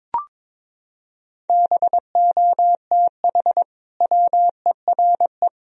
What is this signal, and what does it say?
Morse code, heard as a keyed tone, about 21 wpm, 700 Hz. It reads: BOTH WERE
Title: Sentences from Top 1000 Words - Review Part 2 - 22wpm